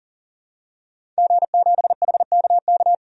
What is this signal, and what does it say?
Morse code, heard as a keyed tone, about 40 wpm, 700 Hz. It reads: G7HKK
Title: DXCC-100 Call Signs - 5 Characters - 40wpm